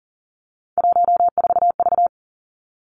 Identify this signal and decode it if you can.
Morse code, heard as a keyed tone, about 40 wpm, 700 Hz. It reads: 14V
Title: Three Challenging Letter-Number Combinations 40wpm